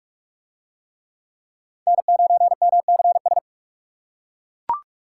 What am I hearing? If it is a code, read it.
Morse code, heard as a keyed tone, about 41 wpm, 700 Hz. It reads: N9MCS